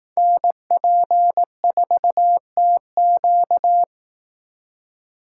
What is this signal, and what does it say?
Morse code, heard as a keyed tone, about 18 wpm, 700 Hz. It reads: NP4TQ
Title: Speed Racing - USA 2x2 Call Signs 15wpm